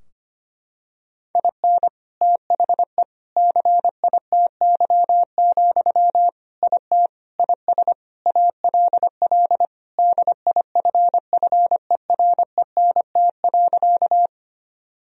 Morse code, heard as a keyed tone, about 25 wpm, 700 Hz. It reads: IN THE CITY, IT IS ALL DIFFERENT.
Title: Sentences from Top 1000 Words - Review Part 3 - 25wpm